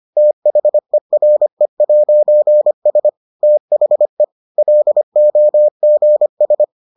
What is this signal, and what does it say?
Morse code, heard as a keyed tone, about 25 wpm, 600 Hz. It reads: THERE'S THE LOGS